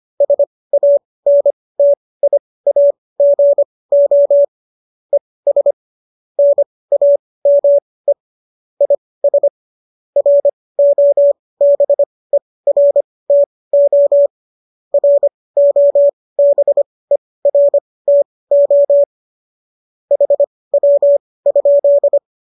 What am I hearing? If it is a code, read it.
Morse code, heard as a keyed tone, about 25 wpm, spaced as for 18 wpm, 570 Hz. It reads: SANTIAGO ES NAME IS ROBERTO ROBERTO HW?